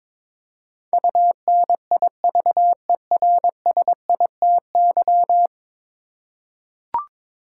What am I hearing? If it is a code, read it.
Morse code, heard as a keyed tone, about 21 wpm, 700 Hz. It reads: UNIVERSITY
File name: practice / 22wpm / 10 Letter Words - First 3 Letters Given - 22wpm